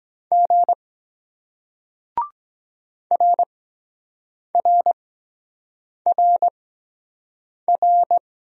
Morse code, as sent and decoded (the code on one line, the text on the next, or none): --. / .-. .-. .-. .-.
G RRRR